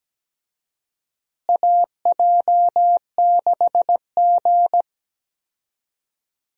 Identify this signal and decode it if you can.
Morse code, heard as a keyed tone, about 17 wpm, 700 Hz. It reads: AJ6G
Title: USA Call Signs - 4 Characters - Repeated 2x - 17wpm